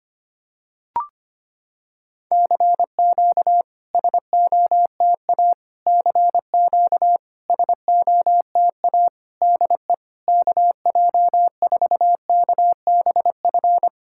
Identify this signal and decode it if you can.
Morse code, heard as a keyed tone, about 25 wpm, 700 Hz. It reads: CQ SOTA CQ SOTA DE KJ4KBF
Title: SOTA QSO Practice 25wpm